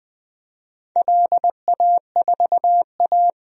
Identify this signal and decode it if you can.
Morse code, heard as a keyed tone, about 20 wpm, 700 Hz. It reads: LA4A
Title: DXCC-100 Call Signs - 4 Characters - Repeated 3x - 20wpm